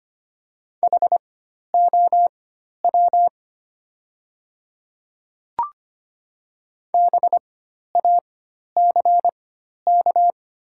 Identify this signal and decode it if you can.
Morse code, heard as a keyed tone, about 25 wpm, spaced as for 12 wpm, 690 Hz. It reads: HOW BACK
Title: 4x Character Spacing - Top 200 Words - 25wpm